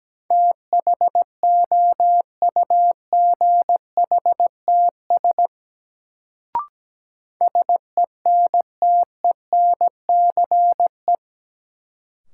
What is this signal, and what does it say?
Morse code, heard as a keyed tone, about 17 wpm, 700 Hz. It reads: THOUGHTS SENTENCE
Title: Sentences from Top 1000 Words - Review Part 1 - New Word Drill - 17wpm